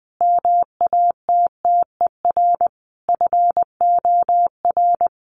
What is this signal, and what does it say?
Morse code, heard as a keyed tone, about 20 wpm, 700 Hz. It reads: MATTER FOR
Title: Sentences from Top 1400 Words 20wpm